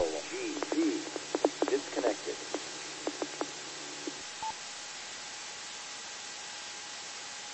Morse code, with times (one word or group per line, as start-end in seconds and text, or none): none